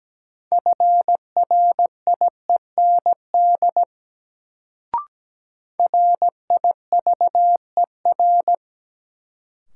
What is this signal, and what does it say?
Morse code, heard as a keyed tone, about 17 wpm, 700 Hz. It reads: FRIEND RIVER
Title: Sentences from Top 300 Words - New Word Drill - 17wpm